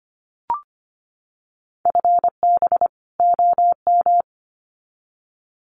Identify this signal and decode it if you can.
Morse code, heard as a keyed tone, about 25 wpm, 700 Hz. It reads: FB OM
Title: Common QSO Phrases 25wpm